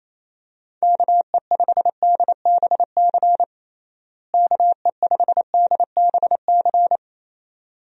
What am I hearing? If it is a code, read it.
Morse code, heard as a keyed tone, about 27 wpm, 700 Hz. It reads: KE5DBC KE5DBC
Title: USA Call Signs - 6 Characters - Repeated 2x - 28wpm